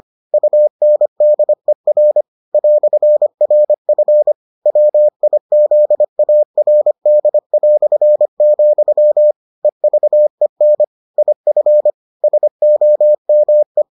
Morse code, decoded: UNDER "RF WIZARD", EVEN IF SOME

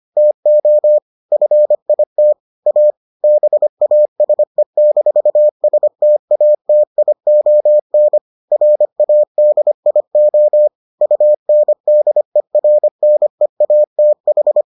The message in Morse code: TO FIT A BASE-STATION RADIO UNDERNEATH

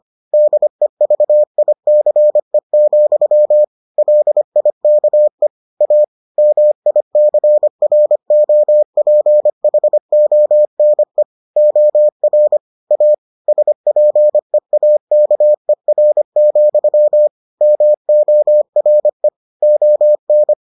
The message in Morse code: DEVICE, LIKE A MICROPHONE OR A SPEAKER, MORE ON

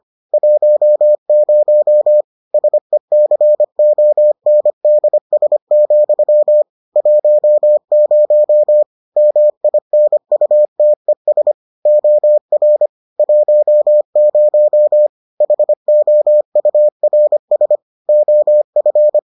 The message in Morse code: 10 SECONDS, 10 MINUTES OR 10 HOURS OF

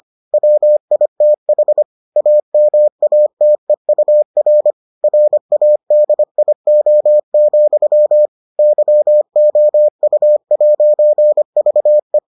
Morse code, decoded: WITH AMATEUR RADIO, YOU'VE

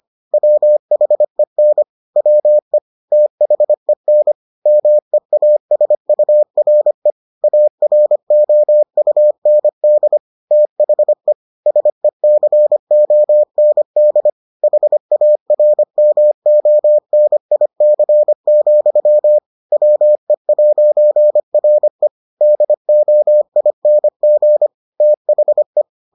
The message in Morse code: WHEN WE THEN MEASURE AROUND THE SECOND HARMONIC, WE'RE DOING THE

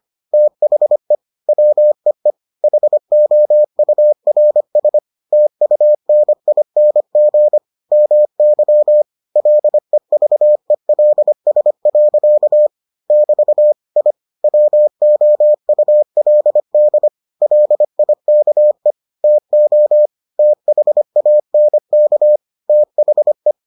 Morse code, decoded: THE WEE HOURS TUNING MY LEVELS. <BT> I WOULD LIKE TO THANK THE

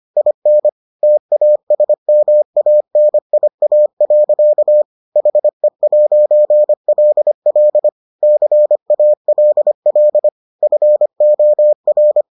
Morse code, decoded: IN TASMANIA. HE'LL CALL FOR

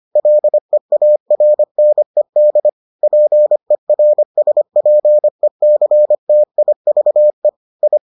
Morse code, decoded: LEARNED PERSPECTIVE I

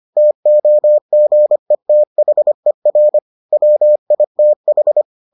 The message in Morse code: TOGETHER WITH

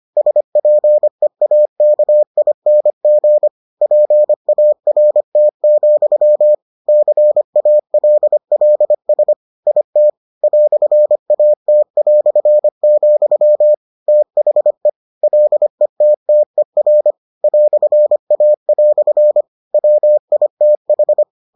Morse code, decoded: SPEAKING PART, CALLS IT "AT", THE LETTER "A" WITH

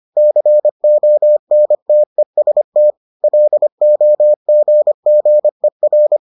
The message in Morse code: CONTEST LOGGER

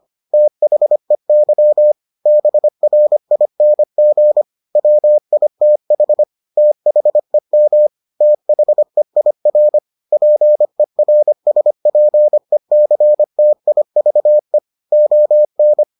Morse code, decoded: THEY BRING WITH THEM THEIR PERSPECTIVE ON

